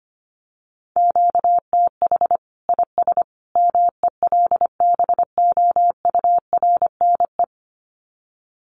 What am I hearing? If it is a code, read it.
Morse code, heard as a keyed tone, about 25 wpm, 700 Hz. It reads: QTH IS MELBOURNE
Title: Common QSO Phrases 25wpm